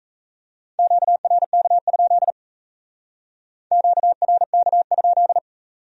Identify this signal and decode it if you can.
Morse code, heard as a keyed tone, about 35 wpm, 700 Hz. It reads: QRK? QRK?